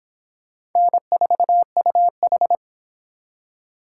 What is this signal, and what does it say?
Morse code, heard as a keyed tone, about 23 wpm, 700 Hz. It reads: N4UH